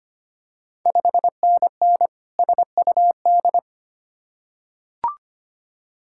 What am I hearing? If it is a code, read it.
Morse code, heard as a keyed tone, about 25 wpm, 700 Hz. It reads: 5NN SUD
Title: Ontario QSO Party Exchange 25wpm